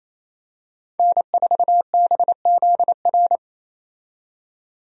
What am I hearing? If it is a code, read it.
Morse code, heard as a keyed tone, about 28 wpm, 700 Hz. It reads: N4BZR
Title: USA Call Signs - 5 Characters - 28wpm